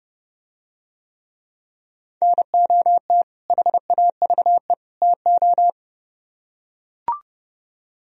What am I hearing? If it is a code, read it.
Morse code, heard as a keyed tone, about 30 wpm, 700 Hz. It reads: NOT HAVE TO